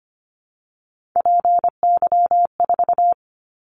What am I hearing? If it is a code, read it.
Morse code, heard as a keyed tone, about 25 wpm, 700 Hz. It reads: PY4
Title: Callsign Prefixes for Top 50 DXCC - 25wpm